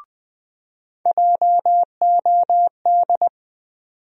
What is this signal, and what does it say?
Morse code, heard as a keyed tone, about 20 wpm, 700 Hz. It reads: JOD
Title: USA Call Sign Postfixes - 3 Letters 20wpm